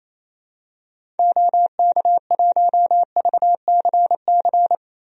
Morse code, heard as a keyed tone, about 28 wpm, 700 Hz. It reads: OK1VCC
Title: DXCC-100 Call Signs - 6 Characters - Repeated 2x - 28wpm